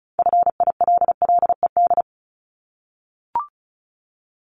Morse code, heard as a keyed tone, about 36 wpm, 700 Hz. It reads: FILLED
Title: Multi-letter Phonemes 35wpm